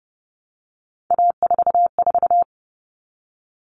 Morse code, heard as a keyed tone, about 30 wpm, 700 Hz. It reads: A44
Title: All Cut Numbers - 3-Digit 30wpm